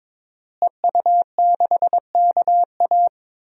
Morse code, heard as a keyed tone, about 22 wpm, 700 Hz. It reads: EU6KA
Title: DXCC-100 Call Signs - 5 Characters - Repeated 2x - 22wpm